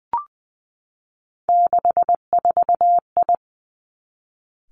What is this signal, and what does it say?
Morse code, heard as a keyed tone, about 20 wpm, 700 Hz. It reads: 64I
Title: Three Challenging Letter-Number Combinations 20wpm